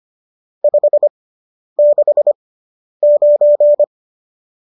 Morse code, heard as a keyed tone, about 25 wpm, spaced as for 10 wpm, 590 Hz. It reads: 569